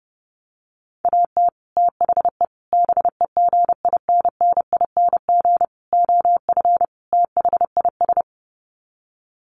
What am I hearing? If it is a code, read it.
Morse code, heard as a keyed tone, about 30 wpm, 700 Hz. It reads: AT THE BEGINNING OF THIS
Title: Sets of 5 Words Encore 30wpm